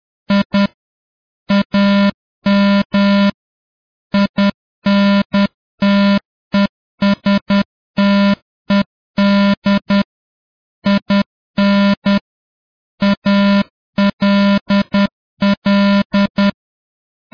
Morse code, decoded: I AM INTESTED IN ALL